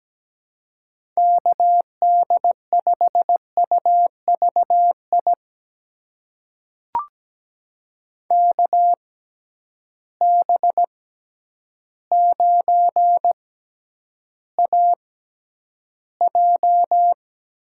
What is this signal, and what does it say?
Morse code, heard as a keyed tone, about 17 wpm, spaced as for 9 wpm, 700 Hz. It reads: KD5UVI KB9AJ
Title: USA Call Signs - 6 Characters - 6x Character Spacing - 17wpm